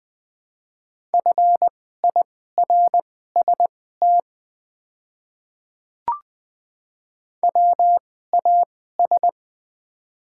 Morse code, heard as a keyed tone, about 20 wpm, spaced as for 14 wpm, 700 Hz. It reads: FIRST WAS
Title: 2x Character Spacing - Top 100 Words - 20wpm